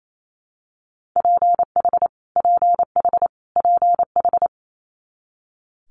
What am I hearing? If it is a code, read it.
Morse code, heard as a keyed tone, about 28 wpm, 700 Hz. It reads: PH PH PH
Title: Multi-letter Phonemes Repeated 3x 28wpm